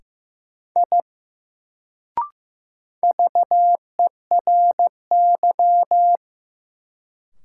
Morse code, heard as a keyed tone, about 15 wpm, 700 Hz. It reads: I VERY
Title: Sentences from Top 200 Words - New Word Drill - 15wpm